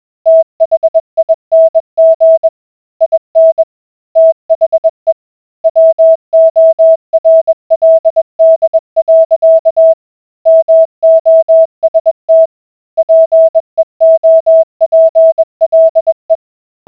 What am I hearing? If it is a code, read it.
Morse code, heard as a keyed tone, about 21 wpm, 650 Hz. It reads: THING IN THE WORLD. MOST PEOPLE